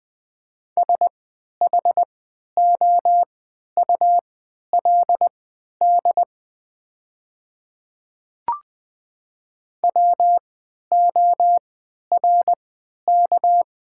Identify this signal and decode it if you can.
Morse code, heard as a keyed tone, about 20 wpm, spaced as for 11 wpm, 690 Hz. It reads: SHOULD WORK
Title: 3x Character Spacing - Top 300 Words - 20wpm